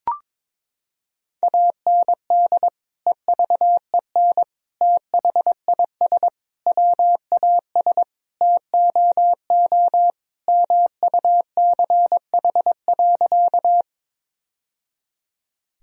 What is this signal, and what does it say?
Morse code, heard as a keyed tone, about 22 wpm, 700 Hz. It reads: AND EVEN THIS WAS TOO MUCH.
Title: Sentences from Top 1000 Words - Review Part 1 - 22wpm